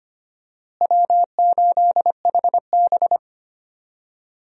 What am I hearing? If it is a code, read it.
Morse code, heard as a keyed tone, about 25 wpm, 700 Hz. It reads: W8HB
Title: USA Call Signs - 4 Characters - Repeated 3x - 25wpm